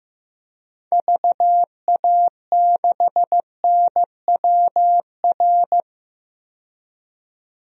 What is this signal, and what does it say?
Morse code, heard as a keyed tone, about 15 wpm, 700 Hz. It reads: VA6NWR
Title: DXCC-100 Call Signs - 6 Characters - 15wpm